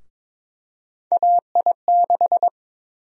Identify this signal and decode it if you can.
Morse code, heard as a keyed tone, about 22 wpm, 700 Hz. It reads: AI6
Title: USA Call Sign Prefixes - 22wpm